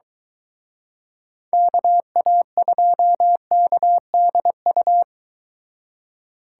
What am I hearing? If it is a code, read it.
Morse code, heard as a keyed tone, about 22 wpm, 700 Hz. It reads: KA2KDU